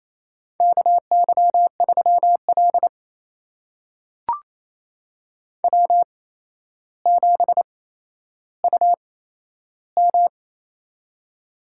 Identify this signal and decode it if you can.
Morse code, heard as a keyed tone, about 28 wpm, spaced as for 12 wpm, 700 Hz. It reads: KY3L W7UM
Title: USA Call Signs - 4 Characters - 8x Character Spacing - 28wpm